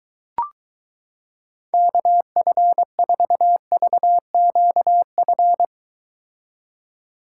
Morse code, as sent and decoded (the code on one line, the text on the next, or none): -.- ..-. ....- ...- --.- ..-.
KF4VQF